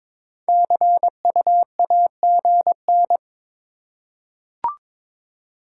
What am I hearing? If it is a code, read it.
Morse code, heard as a keyed tone, about 22 wpm, 690 Hz. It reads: CU AGN
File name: practice / 22wpm / Top 20 QSO Elements - 2x Character Spacing - 22wpm